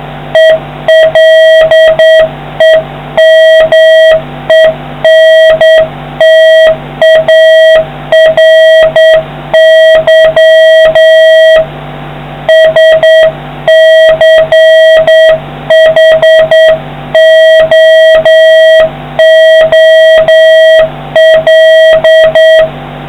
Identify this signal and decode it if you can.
Morse code, heard as a keyed tone, about 9 wpm, 1900 Hz, with noise. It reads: ELEMENTARY SCHOOL